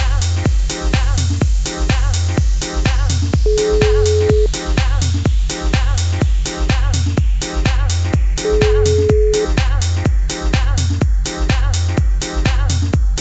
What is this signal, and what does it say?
Ringback tone (Europe, 425 Hz), with noise.